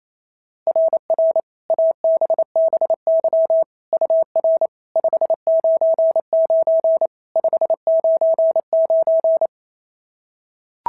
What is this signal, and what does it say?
Morse code, heard as a keyed tone, about 28 wpm, 660 Hz. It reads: RR ABBY UR 599 599
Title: SOTA QSO Practice 28wpm